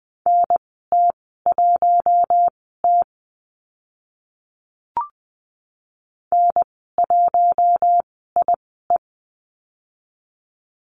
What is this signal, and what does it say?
Morse code, heard as a keyed tone, about 20 wpm, spaced as for 14 wpm, 700 Hz. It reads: NT1T N1IE
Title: 042 Lesson 8g - Number 1 - Call Signs 20wpm